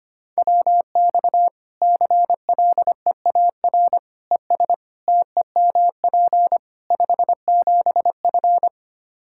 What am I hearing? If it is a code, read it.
Morse code, heard as a keyed tone, about 25 wpm, 700 Hz. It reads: WX CLEAR ES TEMP 57F
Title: Practice QSO - Exchange 2 - Info - 25wpm